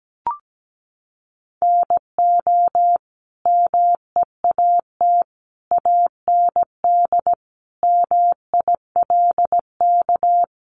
Morse code, heard as a keyed tone, about 17 wpm, 700 Hz. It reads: NO MEAT AND MILK
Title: Sentences from Top 1500 Words 17wpm